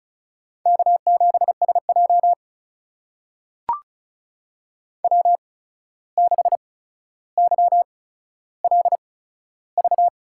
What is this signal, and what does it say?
Morse code, heard as a keyed tone, about 36 wpm, spaced as for 14 wpm, 700 Hz. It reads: K7SJ W6YLV
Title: USA Call Signs - 8x Character Spacing - 35wpm